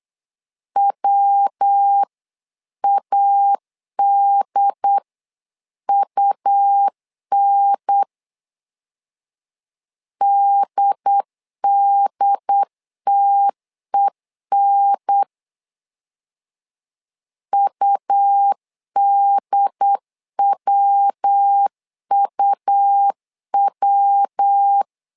Morse code, heard as a keyed tone, about 8 wpm, 800 Hz. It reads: WADUN DDTEN UDWUW
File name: LEZIONI / 3 MWRO / 3 MW 8wpm 2501